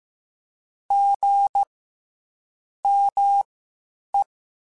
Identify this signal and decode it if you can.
Morse code, heard as a keyed tone, about 15 wpm, spaced as for 7 wpm, 800 Hz. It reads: GME
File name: CW Comparisons / CW Comparisons-mp3s / CW Comparisons Broken Characters @13